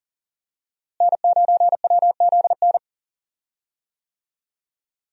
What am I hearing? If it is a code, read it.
Morse code, heard as a keyed tone, about 37 wpm, 700 Hz. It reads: N9WZN
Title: USA Call Signs - 5 Characters - Repeated 3x - 40wpm